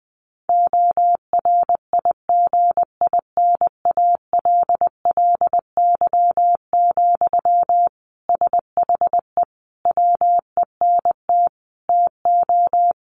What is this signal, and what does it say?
Morse code, heard as a keyed tone, about 20 wpm, 700 Hz. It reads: ORIGINALLY, SHE WENT TO
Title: Sentences from Top 1500 Words 20wpm